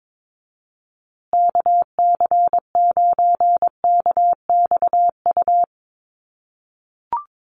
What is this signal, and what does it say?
Morse code, heard as a keyed tone, about 22 wpm, 700 Hz. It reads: KC9KXU
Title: USA Call Signs Repeated 3x 22wpm